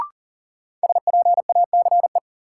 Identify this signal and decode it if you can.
Morse code, heard as a keyed tone, about 40 wpm, 700 Hz. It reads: SPACE
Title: Sentences from Top 1000 Words - Review Part 5 - New Word Drill - 40wpm